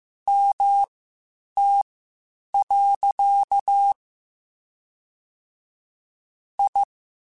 Morse code, heard as a keyed tone, about 15 wpm, spaced as for 8 wpm, 800 Hz. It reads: MT. I
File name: (spaced) / QSOs Spaced-mp3s / WD0BGP 1 @13 sp